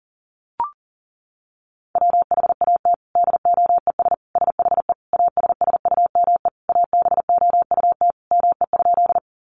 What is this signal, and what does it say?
Morse code, heard as a keyed tone, about 40 wpm, 700 Hz. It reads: WHAT DOES SHE ASSUME ABOUT ME?
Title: Sentences from Top 1500 Words 40wpm